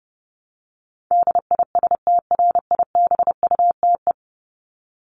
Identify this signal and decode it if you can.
Morse code, heard as a keyed tone, about 30 wpm, 700 Hz. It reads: DISTRIBUTE
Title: Top 1501-1750 Words 30wpm